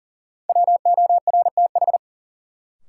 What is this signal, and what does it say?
Morse code, heard as a keyed tone, about 40 wpm, 700 Hz. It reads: WORTH
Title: Sentences from Top 800 Words - New Word Drill - 40wpm